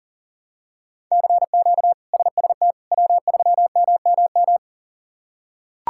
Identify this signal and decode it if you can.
Morse code, heard as a keyed tone, about 40 wpm, 700 Hz. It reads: CQ SST W3MMM
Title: K1USN SST QSO Practice - 40wpm